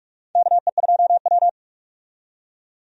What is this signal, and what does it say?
Morse code, heard as a keyed tone, about 48 wpm, 700 Hz. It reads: KE2W
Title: USA Call Signs - 4 Characters - 45wpm